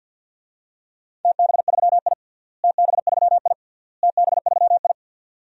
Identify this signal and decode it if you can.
Morse code, heard as a keyed tone, about 50 wpm, 700 Hz. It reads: TB3I TB3I TB3I
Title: DXCC-100 Call Signs - 4 Characters - Repeated 3x - 50wpm